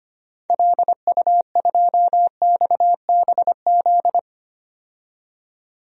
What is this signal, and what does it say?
Morse code, heard as a keyed tone, about 25 wpm, 690 Hz. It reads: LU2XBZ
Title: DXCC-100 Call Signs - 6 Characters - 25wpm